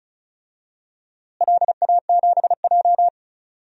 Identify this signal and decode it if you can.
Morse code, heard as a keyed tone, about 36 wpm, 700 Hz. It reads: LA7J
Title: DXCC-100 Call Signs - 4 Characters - Repeated 3x - 35wpm